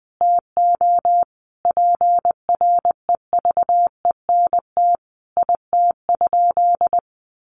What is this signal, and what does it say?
Morse code, heard as a keyed tone, about 19 wpm, 700 Hz. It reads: TO PREVENT IT?
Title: Sentences from Top 1100 Words 20wpm